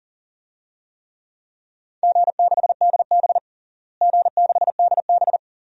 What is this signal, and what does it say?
Morse code, heard as a keyed tone, about 40 wpm, 700 Hz. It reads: G6DB G6DB
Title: DXCC-100 Call Signs - 4 Characters - Repeated 2x - 40wpm